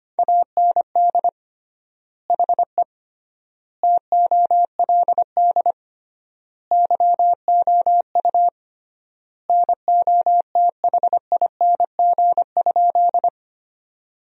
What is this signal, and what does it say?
Morse code, heard as a keyed tone, about 25 wpm, 700 Hz. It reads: AND HE TOLD YOU NOTHING?
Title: 3x Word-Spacing - Sentences from Top 500 Words - 25wpm